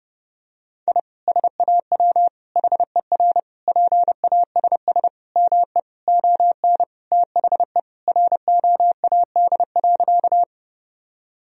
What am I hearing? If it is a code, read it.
Morse code, heard as a keyed tone, about 30 wpm, 700 Hz. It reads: I SAW HER PASS ME ON THE ROAD.
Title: Sentences from Top 1000 Words - Review Part 2 - 30wpm